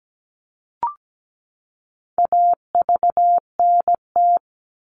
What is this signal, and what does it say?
Morse code, heard as a keyed tone, about 17 wpm, 700 Hz. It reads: AVNT
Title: All Cut Numbers - 4-Digit 17wpm